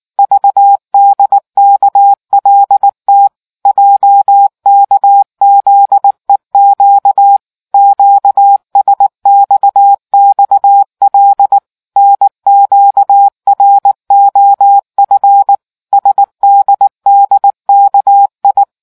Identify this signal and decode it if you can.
Morse code, heard as a keyed tone, about 19 wpm, 800 Hz. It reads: VDKLT JKZEQ QSXXL NQROF SDDKI